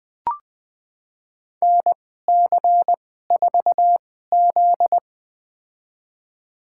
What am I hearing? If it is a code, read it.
Morse code, heard as a keyed tone, about 21 wpm, spaced as for 15 wpm, 700 Hz. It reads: NC4Z
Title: USA Call Signs - 2x Character Spacing - 20wpm